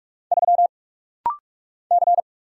Morse code, heard as a keyed tone, about 46 wpm, 700 Hz. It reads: 3 C
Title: Single Letter-Number - Rapid-Fire 45wpm